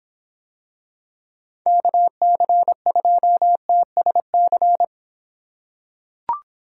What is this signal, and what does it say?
Morse code, heard as a keyed tone, about 26 wpm, 700 Hz. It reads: KC2TSC